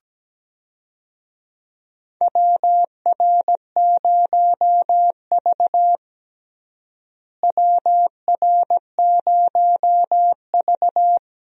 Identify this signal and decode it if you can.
Morse code, heard as a keyed tone, about 17 wpm, 700 Hz. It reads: WR0V WR0V
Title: USA Call Signs - 4 Characters - Repeated 2x - 17wpm